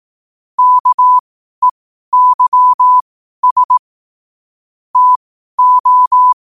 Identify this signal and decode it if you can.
Morse code, heard as a keyed tone, about 18 wpm, spaced as for 12 wpm, 1000 Hz. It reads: KEYS TO
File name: qotdfiles / 12wpm